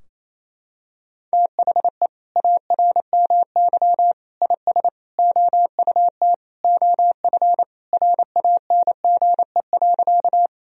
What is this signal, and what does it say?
Morse code, heard as a keyed tone, about 28 wpm, 700 Hz. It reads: THE ARMY IS OUT OF RANGE.
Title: Sentences from Top 1000 Words - Review Part 4 - 28wpm